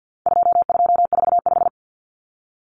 Morse code, heard as a keyed tone, about 50 wpm, 700 Hz. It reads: 2245
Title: Numbers - 4-Digit 50wpm